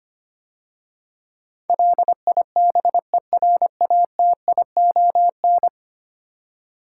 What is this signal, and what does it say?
Morse code, heard as a keyed tone, about 25 wpm, 700 Hz. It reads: LIBERATION
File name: 10 Letter Words - First 3 Letters Given - 25wpm